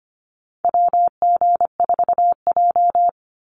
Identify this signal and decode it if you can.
Morse code, heard as a keyed tone, about 25 wpm, 700 Hz. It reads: WG4J
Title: USA Call Signs 25wpm